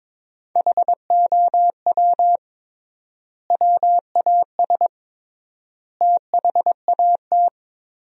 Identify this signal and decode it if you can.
Morse code, heard as a keyed tone, about 22 wpm, 700 Hz. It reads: HOW WAS THAT
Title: 3x Word-Spacing - Sentences from Top 500 Words - 22wpm